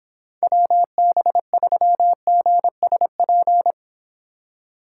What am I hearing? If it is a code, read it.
Morse code, heard as a keyed tone, about 26 wpm, 700 Hz. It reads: WB3GSP